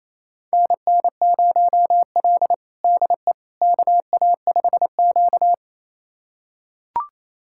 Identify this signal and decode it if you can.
Morse code, heard as a keyed tone, about 28 wpm, 700 Hz. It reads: NN0L DE KA5Q
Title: USA Callsign DE Callsign - 4 Characters - 28wpm